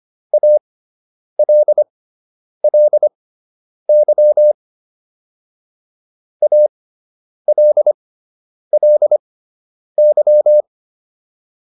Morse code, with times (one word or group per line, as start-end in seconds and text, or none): 0.3-4.5 ALLY
6.4-10.6 ALLY